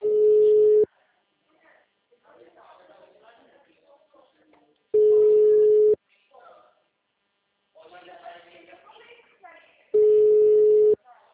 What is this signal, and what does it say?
Ringback tone (Europe, 425 Hz), with speech.